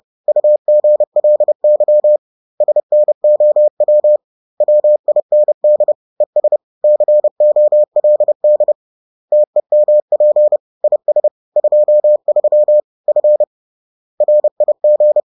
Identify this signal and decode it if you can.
Morse code, heard as a keyed tone, about 30 wpm, 600 Hz. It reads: UGLY SNOW WIND ES COLD TEMP IS 23 F RIG
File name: 30wpm-4